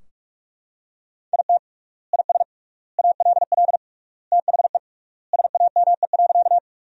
Morse code, heard as a keyed tone, about 41 wpm, 700 Hz. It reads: IT IS ALL THE SAME.
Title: 3x Word-Spacing - Sentences from Top 200 Words - 45wpm